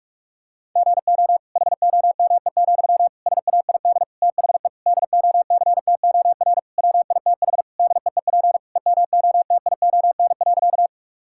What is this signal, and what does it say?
Morse code, heard as a keyed tone, about 46 wpm, 700 Hz. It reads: GO HOME, SAID THE DOCTOR WITH DEEP EMOTION.
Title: Sentences from Top 1600 Words 45wpm